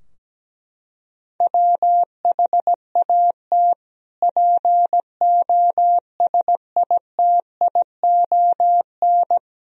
Morse code, heard as a keyed tone, about 17 wpm, 700 Hz. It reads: WHAT POSITION
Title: Sentences from Top 1000 Words - Review Part 3 - 17wpm